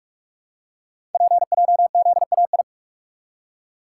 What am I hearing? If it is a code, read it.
Morse code, heard as a keyed tone, about 44 wpm, 700 Hz. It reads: PJZAI
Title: Code Groups - 5 Letters - 45wpm